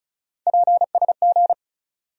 Morse code, heard as a keyed tone, about 34 wpm, 700 Hz. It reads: PSG
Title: USA Call Sign Postfixes - 3 Letters 35wpm